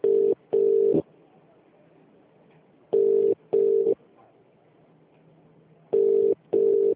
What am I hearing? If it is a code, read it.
Ringback tone (UK), 400 + 450 Hz.